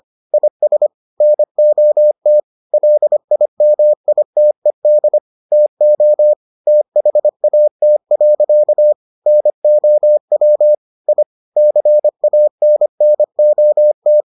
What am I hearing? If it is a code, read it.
Morse code, heard as a keyed tone, about 25 wpm, 600 Hz. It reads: IS NOT LIMITED TO THAT. NOW I CANNOT